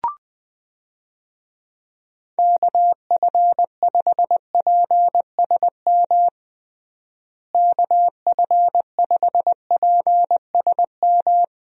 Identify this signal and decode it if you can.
Morse code, heard as a keyed tone, about 20 wpm, 700 Hz. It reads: KF5PSM KF5PSM
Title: USA Call Signs - 6 Characters - Repeated 2x - 20wpm